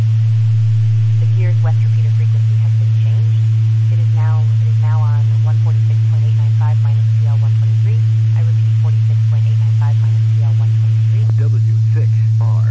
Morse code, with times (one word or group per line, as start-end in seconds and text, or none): none